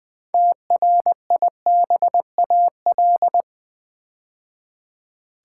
Morse code, heard as a keyed tone, about 20 wpm, 700 Hz. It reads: TRIBAL
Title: 6 Letter Words - First 3 Letters Given - 20wpm